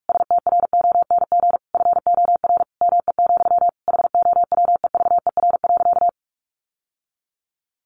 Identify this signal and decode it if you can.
Morse code, heard as a keyed tone, about 44 wpm, 700 Hz. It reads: STRONG FOR ME, HOWEVER.
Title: Sentences from Top 600 Words 45wpm